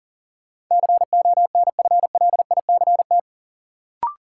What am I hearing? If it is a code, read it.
Morse code, heard as a keyed tone, about 40 wpm, 700 Hz. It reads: CONFLICT
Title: Sentences from Top 1300 Words - New Word Drill - 40wpm